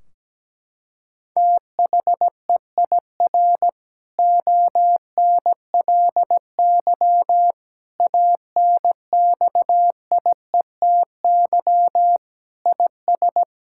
Morse code, heard as a keyed tone, about 17 wpm, 700 Hz. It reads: THEIR ONLY ANXIETY IS
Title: Sentences from Top 1600 Words 17wpm